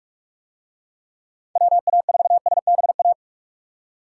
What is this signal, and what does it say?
Morse code, heard as a keyed tone, about 44 wpm, 700 Hz. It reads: WA4SBA